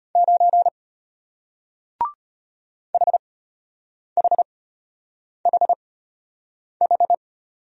Morse code, heard as a keyed tone, about 33 wpm, spaced as for 8 wpm, 700 Hz. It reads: 9 HHHH